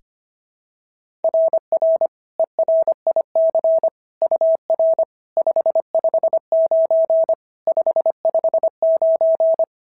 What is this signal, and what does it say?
Morse code, heard as a keyed tone, about 24 wpm, 650 Hz. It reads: RR ERIC UR 559 559
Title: SOTA QSO Practice 25wpm